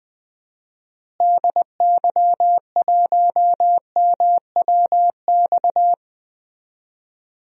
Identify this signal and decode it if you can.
Morse code, heard as a keyed tone, about 20 wpm, 700 Hz. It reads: DY1MWX
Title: DXCC-100 Call Signs - 6 Characters - Repeated 3x - 20wpm